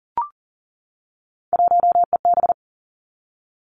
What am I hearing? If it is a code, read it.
Morse code, heard as a keyed tone, about 40 wpm, 700 Hz. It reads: 1EB
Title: Three Challenging Letter-Number Combinations 40wpm